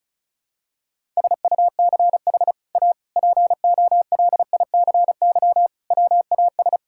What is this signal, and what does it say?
Morse code, heard as a keyed tone, about 34 wpm, 700 Hz. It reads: SUCH A POLICY WAS
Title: Sentences from Top 1000 Words - Review Part 5 - 35wpm